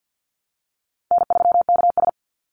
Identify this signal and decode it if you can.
Morse code, heard as a keyed tone, about 49 wpm, 700 Hz. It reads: N3KS
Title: Easy Call Signs Repeated 3x - 50wpm